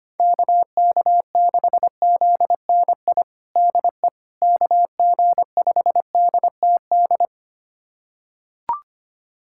Morse code, heard as a keyed tone, about 25 wpm, 700 Hz. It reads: KK6ZNI DE KG5DTD
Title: USA Callsign DE Callsign - 6 Characters - 25wpm